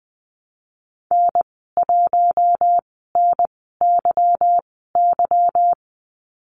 Morse code, heard as a keyed tone, about 20 wpm, spaced as for 14 wpm, 700 Hz. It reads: N1NYY
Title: 173 Lesson 28g - Number 9 - Call Signs 20wpm